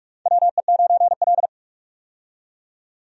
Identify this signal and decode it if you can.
Morse code, heard as a keyed tone, about 44 wpm, 700 Hz. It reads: WE9L
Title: USA Call Signs - 4 Characters - 45wpm